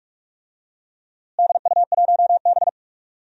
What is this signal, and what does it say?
Morse code, heard as a keyed tone, about 44 wpm, 700 Hz. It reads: DU1B
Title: DXCC-100 Call Signs - 4 Characters - Repeated 3x - 45wpm